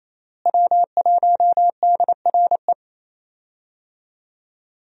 Morse code, heard as a keyed tone, about 28 wpm, 700 Hz. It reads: W1DRE
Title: USA Call Signs - 5 Characters - Repeated 2x - 28wpm